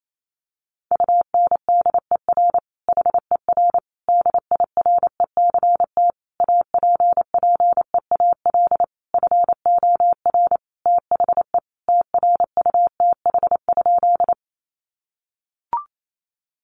Morse code, heard as a keyed tone, about 28 wpm, 700 Hz. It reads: UNDER HER DIRECT APPEAL FOR THE TRUTH?
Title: Sentences from Top 1300 Words 28wpm